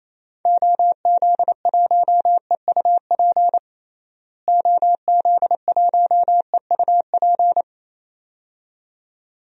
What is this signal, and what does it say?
Morse code, heard as a keyed tone, about 27 wpm, 700 Hz. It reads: OZ1EUP OZ1EUP
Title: DXCC-100 Call Signs - 6 Characters - Repeated 2x - 28wpm